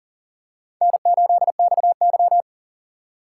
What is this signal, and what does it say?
Morse code, heard as a keyed tone, about 40 wpm, 700 Hz. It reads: N8XY
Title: USA Call Signs - 4x Character Spacing - 40wpm